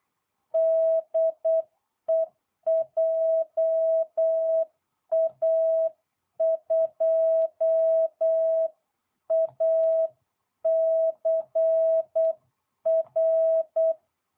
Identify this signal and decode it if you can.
Morse code, heard as a keyed tone, about 8 wpm, 650 Hz. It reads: DE JA2ACR